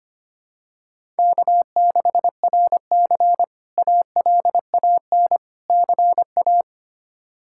Morse code, heard as a keyed tone, about 25 wpm, 700 Hz. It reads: K6RC ALAN CA
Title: K1USN SST QSO Practice - 25wpm